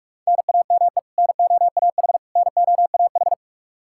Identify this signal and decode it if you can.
Morse code, heard as a keyed tone, about 46 wpm, 700 Hz. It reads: NAME NOAH NOAH
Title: Practice QSO - Exchange 1 - Beginning - 45wpm